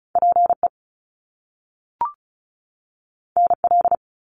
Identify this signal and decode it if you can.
Morse code, heard as a keyed tone, about 36 wpm, 700 Hz. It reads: PE NL
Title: Canadian Provinces and Territories 35wpm